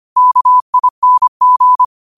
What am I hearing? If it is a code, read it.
Morse code, heard as a keyed tone, about 25 wpm, 1000 Hz. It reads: KING
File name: qotdfiles / 25wpm